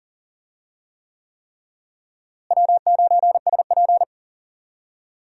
Morse code, heard as a keyed tone, about 40 wpm, 700 Hz. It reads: W9SP
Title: USA Call Signs - 4 Characters - Repeated 3x - 40wpm